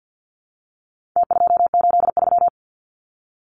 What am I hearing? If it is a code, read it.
Morse code, heard as a keyed tone, about 49 wpm, 700 Hz. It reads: T283
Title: Numbers - Contest Style - 4-Digit 50wpm